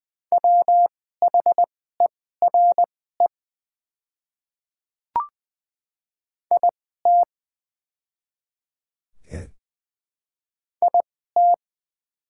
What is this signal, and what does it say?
Morse code, heard as a keyed tone, about 20 wpm, spaced as for 14 wpm, 700 Hz. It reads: WHERE IT IT
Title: 2x Character Spacing - Top 200 Words - 20wpm